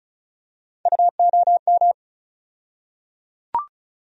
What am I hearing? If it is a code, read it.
Morse code, heard as a keyed tone, about 36 wpm, 700 Hz. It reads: UOM
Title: USA Call Sign Postfixes - 3 Letters 35wpm